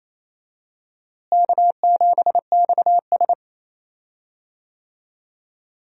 Morse code, heard as a keyed tone, about 28 wpm, 700 Hz. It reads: K7XS
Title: USA Call Signs - 4 Characters - Repeated 2x - 28wpm